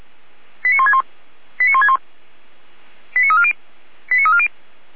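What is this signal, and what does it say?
Five-tone selective call (ZVEI), heard as quick runs of five tones, tones of about 70 ms. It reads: 89161 89161 89340 89340
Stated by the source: ZVEI five-tone selective call received on SDR, from feuerwehr-orlamuende fire brigade server.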